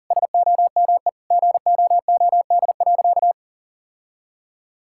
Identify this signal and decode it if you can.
Morse code, heard as a keyed tone, about 37 wpm, 700 Hz. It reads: SOME GOOD.
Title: Sentences from Top 1000 Words - Review Part 6 - 40wpm